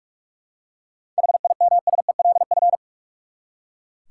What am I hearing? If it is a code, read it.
Morse code, heard as a keyed tone, about 42 wpm, 700 Hz. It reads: HIMSELF
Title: Seven Letter Words - 45wpm